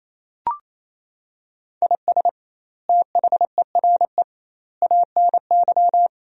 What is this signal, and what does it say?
Morse code, heard as a keyed tone, about 28 wpm, 700 Hz. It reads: IS THERE ANY
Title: 2x Word-Spacing - Sentences from Top 300 Words - 28wpm